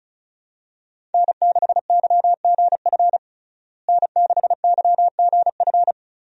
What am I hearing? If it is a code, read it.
Morse code, heard as a keyed tone, about 36 wpm, 700 Hz. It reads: N6YGF N6YGF
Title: USA Call Signs - 5 Characters - Repeated 3x - 35wpm